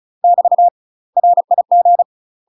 Morse code, heard as a keyed tone, about 36 wpm, 700 Hz. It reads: <BT> RIG
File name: UK MASTER QSOs TRAINING / 35WPM / UK MASTER QSOs 0005